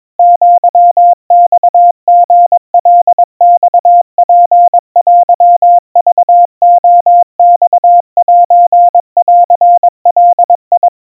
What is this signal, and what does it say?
Morse code (Japanese Wabun code), heard as a keyed tone, about 22 wpm, 700 Hz. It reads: アマリカマツテクレマセンガ
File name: 80 cpm 700 Hz Crispy